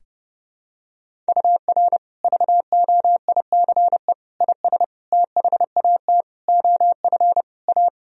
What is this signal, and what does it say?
Morse code, heard as a keyed tone, about 30 wpm, 700 Hz. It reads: UR VOICE IS THAT OF A
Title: Sentences from Top 500 Words and Abbreviations 30wpm